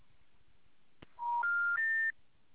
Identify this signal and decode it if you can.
Special information tone.